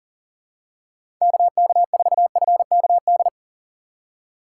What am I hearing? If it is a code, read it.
Morse code, heard as a keyed tone, about 39 wpm, 700 Hz. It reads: KK4FKD
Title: USA Call Signs - 6 Characters - 2x Character Spacing - 40wpm